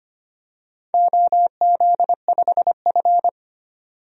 Morse code, heard as a keyed tone, about 25 wpm, 700 Hz. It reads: OZ5F
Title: DXCC-100 Call Signs - 4 Characters - 25wpm